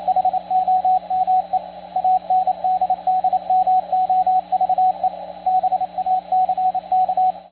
Morse code, heard as a keyed tone, about 27 wpm, 740 Hz, with radio noise. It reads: HOME ANDDMOVE BACK